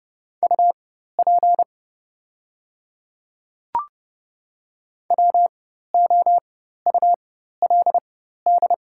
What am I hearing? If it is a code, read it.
Morse code, heard as a keyed tone, about 30 wpm, spaced as for 14 wpm, 700 Hz. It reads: UP WOULD
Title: 4x Character Spacing - Top 200 Words - 30wpm